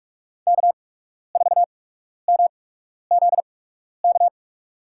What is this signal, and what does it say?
Morse code, heard as a keyed tone, about 41 wpm, spaced as for 12 wpm, 700 Hz. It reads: K4MZK
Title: USA Call Signs - 5 Characters - 8x Character Spacing - 45wpm